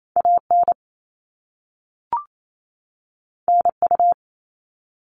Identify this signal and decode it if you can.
Morse code, heard as a keyed tone, about 28 wpm, 700 Hz. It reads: AN NU
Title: All Cut Numbers - 2-Digit 28wpm